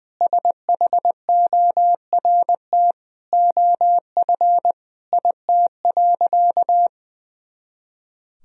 Morse code, heard as a keyed tone, about 21 wpm, 700 Hz. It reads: SHORT OF IT.
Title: Sentences from Top 400 Words and Abbreviations 20wpm